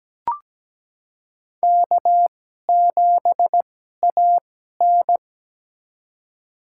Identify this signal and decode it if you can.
Morse code, heard as a keyed tone, about 17 wpm, spaced as for 12 wpm, 700 Hz. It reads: K7AN
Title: USA Call Signs - 4 Characters - 2x Character Spacing - 17wpm